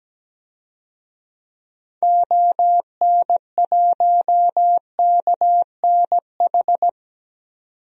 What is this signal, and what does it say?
Morse code, heard as a keyed tone, about 17 wpm, 700 Hz. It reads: ON1KNH
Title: DXCC-100 Call Signs - 6 Characters - 17wpm